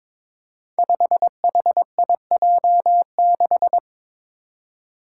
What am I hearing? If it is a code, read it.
Morse code, heard as a keyed tone, about 22 wpm, 700 Hz. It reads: 5HIJ6
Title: Five Challenging Letter-Number Combinations 22wpm